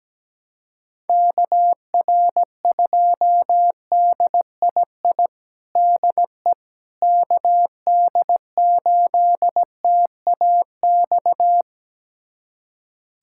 Morse code, heard as a keyed tone, about 17 wpm, 700 Hz. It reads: KR2DII DE KD8TAX
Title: USA Callsign DE Callsign 17wpm